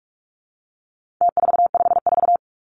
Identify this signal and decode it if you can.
Morse code, heard as a keyed tone, about 44 wpm, 700 Hz. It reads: T454
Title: Numbers - Contest Style - 4-Digit 45wpm